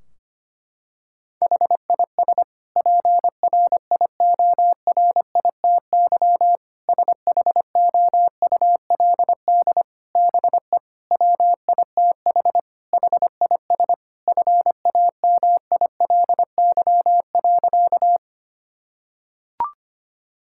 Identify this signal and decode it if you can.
Morse code, heard as a keyed tone, about 25 wpm, 700 Hz. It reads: HIS PRIORITY SHOULD BE WITH HIS FAMILY.
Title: Sentences from Top 1600 Words 25wpm